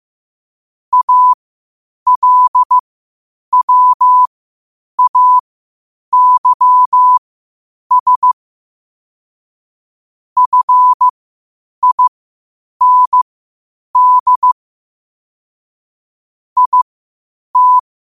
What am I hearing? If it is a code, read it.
Morse code, heard as a keyed tone, about 15 wpm, spaced as for 8 wpm, 1000 Hz. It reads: ALWAYS FIND IT
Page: New quotes every day in morse code at 8 Words per minute.